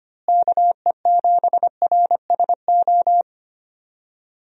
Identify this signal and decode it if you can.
Morse code, heard as a keyed tone, about 25 wpm, 700 Hz. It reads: KE7RSO
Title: USA Call Signs - 6 Characters - Repeated 3x - 25wpm